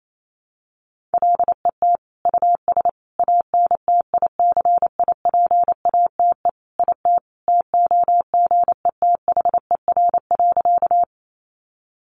Morse code, heard as a keyed tone, about 28 wpm, 700 Hz. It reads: LET US ANTICIPATE IT TOGETHER.
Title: Sentences from Top 1500 Words 28wpm